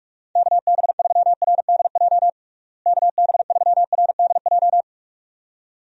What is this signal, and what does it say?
Morse code, heard as a keyed tone, about 44 wpm, 700 Hz. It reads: KB3RDJ KB3RDJ
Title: USA Call Signs - 6 Characters - Repeated 2x - 45wpm